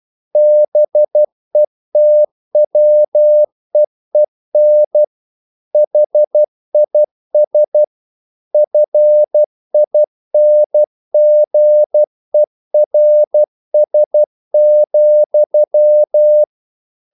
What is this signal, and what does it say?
Morse code, heard as a keyed tone, about 12 wpm, 600 Hz. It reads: BETWEEN HIS FINGERS,
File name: mp3 files / the gardener - 12wpm